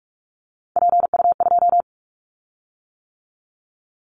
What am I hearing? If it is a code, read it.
Morse code, heard as a keyed tone, about 44 wpm, 700 Hz. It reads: PU2
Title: Callsign Prefixes for Top 100 DXCC - 45wpm